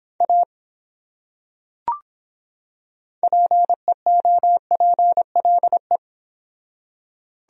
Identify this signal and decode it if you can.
Morse code, heard as a keyed tone, about 25 wpm, 700 Hz. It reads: A PEOPLE